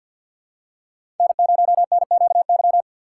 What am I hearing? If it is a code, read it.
Morse code, heard as a keyed tone, about 48 wpm, 690 Hz. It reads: N0NQY
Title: USA Call Signs - 3x Character Spacing - 50wpm